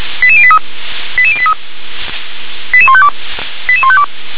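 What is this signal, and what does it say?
Five-tone selective call (ZVEI), heard as quick runs of five tones, tones of about 70 ms. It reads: 88083 88083 88152 88152
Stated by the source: ZVEI five-tone selective call received on SDR.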